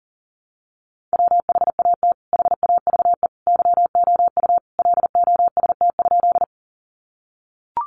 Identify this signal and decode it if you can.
Morse code, heard as a keyed tone, about 40 wpm, 700 Hz. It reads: WHAT HAVE YOU LOST?
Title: Sentences from Top 1400 Words 40wpm